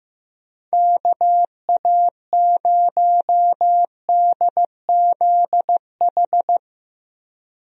Morse code, heard as a keyed tone, about 16 wpm, 700 Hz. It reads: KA0DZH
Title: USA Call Signs - 6 Characters - Repeated 3x - 15wpm